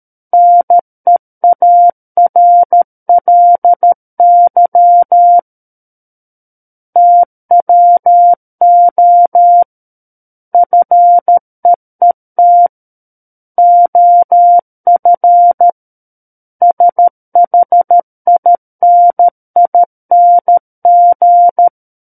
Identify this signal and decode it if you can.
Morse code, heard as a keyed tone, about 13 wpm, 700 Hz. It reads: NEARLY TWO FEET OF SHINING
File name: Book One / 04-Chapter 4 - 13 WPM